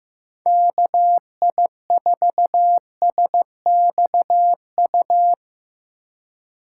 Morse code, heard as a keyed tone, about 15 wpm, 700 Hz. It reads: KI4SXU
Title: USA Call Signs - 6 Characters - 2x Character Spacing - 15wpm